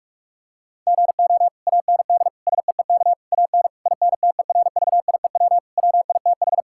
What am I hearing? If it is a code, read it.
Morse code, heard as a keyed tone, about 46 wpm, 700 Hz. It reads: GO AND SEEK AN INTERVIEW WITH
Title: Sentences from Top 1600 Words 45wpm